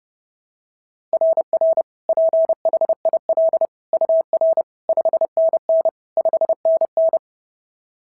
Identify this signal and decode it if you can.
Morse code, heard as a keyed tone, about 30 wpm, 690 Hz. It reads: RR PHIL UR 5NN 5NN
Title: SOTA QSO Practice 30wpm